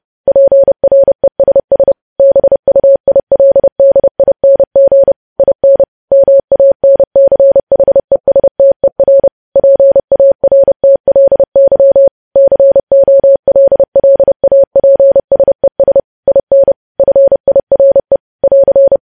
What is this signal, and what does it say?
Morse code, heard as a keyed tone, about 30 wpm, 550 Hz. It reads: PRESS BUILDING IN MANCHESTER PARTLY COLLAPSES IN FIRE <AR>
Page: news headlines in morse code at 30 wpm